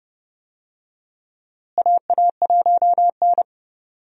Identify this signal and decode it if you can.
Morse code, heard as a keyed tone, about 30 wpm, 700 Hz. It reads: AA1N
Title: USA Call Signs - 4 Characters - Repeated 3x - 30wpm